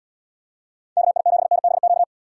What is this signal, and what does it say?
Morse code, heard as a keyed tone, about 78 wpm, 700 Hz. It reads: KI7NBQ